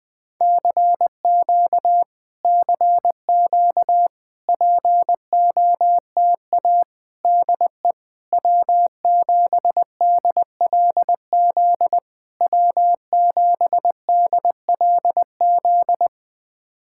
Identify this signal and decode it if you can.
Morse code, heard as a keyed tone, about 20 wpm, 700 Hz. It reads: CQ CQ POTA DE W7DLZ W7DLZ
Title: POTA QSO Practice - 20wpm